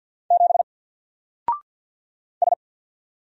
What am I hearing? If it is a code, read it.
Morse code, heard as a keyed tone, about 49 wpm, 700 Hz. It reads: 7 S
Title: Single Character (Letter:Number:Punctuation:Prosign) 50wpm